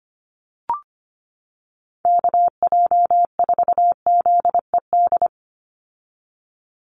Morse code, heard as a keyed tone, about 25 wpm, 700 Hz. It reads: KJ4ZED
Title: USA Call Signs 25wpm